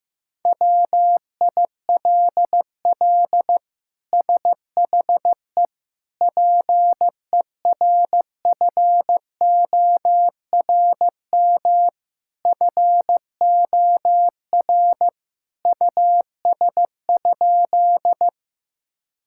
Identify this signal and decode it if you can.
Morse code, heard as a keyed tone, about 15 wpm, 700 Hz. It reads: WILL SHE PERFORM FOR US?
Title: Sentences from Top 1000 Words - Review Part 5 - 15wpm